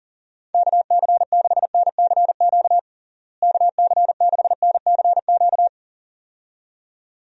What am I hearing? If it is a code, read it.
Morse code, heard as a keyed tone, about 40 wpm, 700 Hz. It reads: KC6NCQ KC6NCQ
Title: USA Call Signs - 6 Characters - Repeated 2x - 40wpm